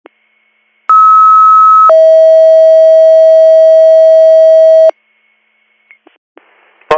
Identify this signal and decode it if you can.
Two-tone page, heard as a one-second tone, then a three-second tone, about 1290 then 640 Hz. It is heard from 0.9 to 4.9 s.